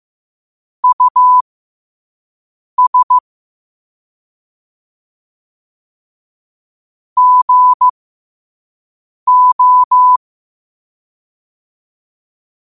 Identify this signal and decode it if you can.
Morse code, heard as a keyed tone, about 15 wpm, spaced as for 5 wpm, 1000 Hz. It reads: US GO
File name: qotdfiles / 5wpm